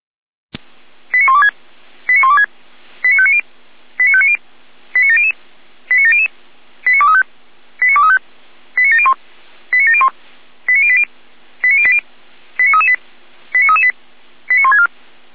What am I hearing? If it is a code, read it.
Five-tone selective call (ZVEI), heard as quick runs of five tones, tones of about 70 ms. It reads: 89126 89126 89590 89590 89700 89700 89235 89235 89701 89701 89080 89080 89338 89338 89164